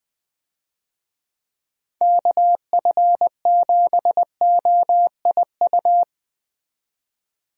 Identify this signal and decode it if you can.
Morse code, heard as a keyed tone, about 20 wpm, 700 Hz. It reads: KF7OIU
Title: USA Call Signs - 6 Characters - 20wpm